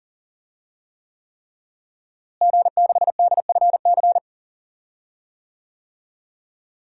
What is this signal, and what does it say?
Morse code, heard as a keyed tone, about 40 wpm, 700 Hz. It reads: G6DFC
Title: DXCC-100 Call Signs - 5 Characters - Repeated 3x - 40wpm